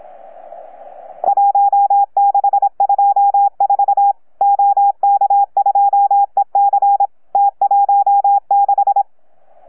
Morse code, heard as a keyed tone, about 26 wpm, 810 Hz. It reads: 1624 OK2EC T16
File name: UP2L-20-CW